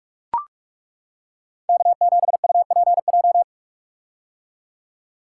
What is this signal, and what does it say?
Morse code, heard as a keyed tone, about 41 wpm, 700 Hz. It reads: K7UPJ
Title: K1USN SST QSO Practice - 45wpm